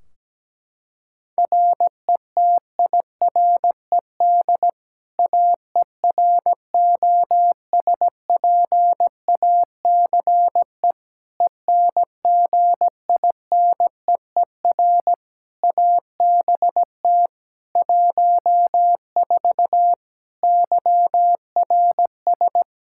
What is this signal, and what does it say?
Morse code, heard as a keyed tone, about 17 wpm, 700 Hz. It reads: RETIRED AEROSPACE ENGINEER ABT 14 YRS
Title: K1USN SST QSO Practice 17wpm